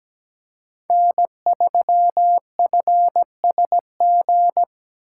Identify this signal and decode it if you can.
Morse code, heard as a keyed tone, about 17 wpm, 700 Hz. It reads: N3FSG
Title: USA Call Signs - 5 Characters - 4x Character Spacing - 17wpm